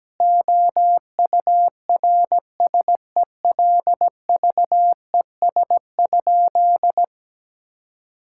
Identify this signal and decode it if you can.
Morse code, heard as a keyed tone, about 17 wpm, 700 Hz. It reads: OURSELVES?
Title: Sentences from Top 1600 Words 17wpm